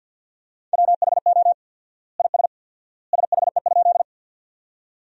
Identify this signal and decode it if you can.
Morse code, heard as a keyed tone, about 46 wpm, 700 Hz. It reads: WHO IS SHE?
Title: 4x Word-Spacing - Sentences from Top 100 Words - 50wpm